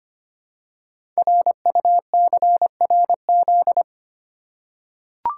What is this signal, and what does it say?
Morse code, heard as a keyed tone, about 25 wpm, 700 Hz. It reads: RUCRZ
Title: Code Groups - 5 Letters - 25wpm